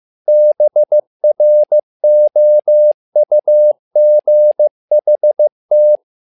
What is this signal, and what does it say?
Morse code, heard as a keyed tone, about 15 wpm, 600 Hz. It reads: BROUGHT
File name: learning center / tower trouble / tower trouble 15wpm